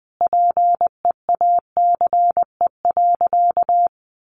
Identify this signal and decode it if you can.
Morse code, heard as a keyed tone, about 20 wpm, 700 Hz. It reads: PEACE.
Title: Sentences from Top 1500 Words 20wpm